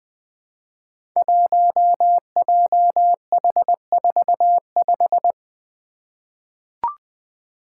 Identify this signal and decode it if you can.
Morse code, heard as a keyed tone, about 20 wpm, 700 Hz. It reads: 1JH45
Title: Five Challenging Letter-Number Combinations 20wpm